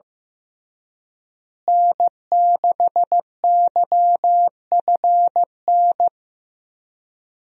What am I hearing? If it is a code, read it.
Morse code, heard as a keyed tone, about 15 wpm, 700 Hz. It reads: N6YFN
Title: USA Call Signs - 5 Characters - Repeated 3x - 15wpm